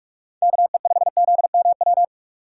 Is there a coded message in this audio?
Yes: Morse code from 0.4 to 2.0 s, heard as a keyed tone, about 59 wpm, 700 Hz. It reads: KE5ZMW